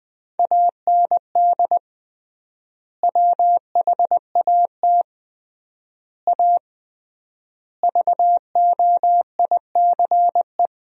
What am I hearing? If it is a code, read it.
Morse code, heard as a keyed tone, about 21 wpm, 700 Hz. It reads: AND WHAT A VOICE
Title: 3x Word-Spacing - Sentences from Top 500 Words - 20wpm